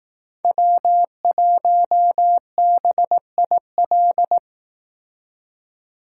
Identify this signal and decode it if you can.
Morse code, heard as a keyed tone, about 18 wpm, 700 Hz. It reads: W1BIL
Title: Speed Racing - USA 1x3 Call Signs 15wpm